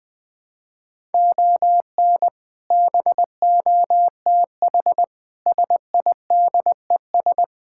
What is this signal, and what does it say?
Morse code, heard as a keyed tone, about 21 wpm, 700 Hz. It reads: ON BOTH SIDES
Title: Build A Phrase - 4-Word Phrases - 20wpm